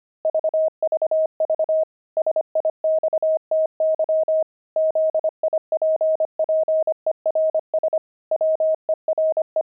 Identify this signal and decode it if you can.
Morse code, heard as a keyed tone, about 25 wpm, 630 Hz. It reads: VVV SIXTY ZIPPERS WERE
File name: pangram-fs-25-cs-25-sixty